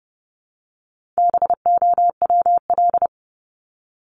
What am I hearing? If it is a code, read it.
Morse code, heard as a keyed tone, about 30 wpm, 700 Hz. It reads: BOWL
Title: Top 1001-1250 Words 30wpm